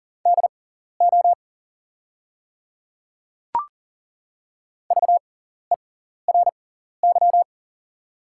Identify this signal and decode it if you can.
Morse code, heard as a keyed tone, about 40 wpm, spaced as for 14 wpm, 690 Hz. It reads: DO VERY
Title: 6x Character Spacing - Top 500 Words - 40wpm